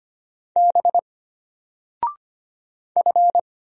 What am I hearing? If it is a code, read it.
Morse code, heard as a keyed tone, about 25 wpm, 700 Hz. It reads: B F
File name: Single-Letter-Number-25wpm